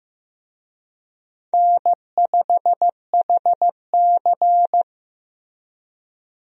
Morse code, heard as a keyed tone, about 15 wpm, 700 Hz. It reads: N5HC
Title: USA Call Signs - 4 Characters - Repeated 3x - 15wpm